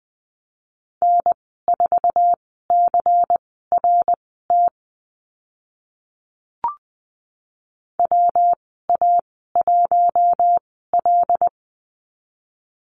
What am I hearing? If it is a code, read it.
Morse code, heard as a keyed tone, about 20 wpm, spaced as for 14 wpm, 700 Hz. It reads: N4CRT WA1L
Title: 140 Lesson 23g - Letter f - Call Signs 20wpm